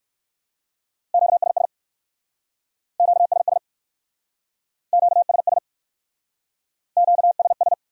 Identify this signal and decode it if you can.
Morse code, heard as a keyed tone, about 58 wpm, 700 Hz. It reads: QSS QSS QSS QSS